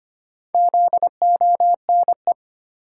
Morse code, heard as a keyed tone, about 25 wpm, 700 Hz. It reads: ZONE
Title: Four Letter Words - 25wpm